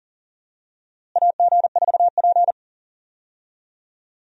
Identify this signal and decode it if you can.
Morse code, heard as a keyed tone, about 40 wpm, 700 Hz. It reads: AG4P
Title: USA Call Signs - 4 Characters - Repeated 3x - 40wpm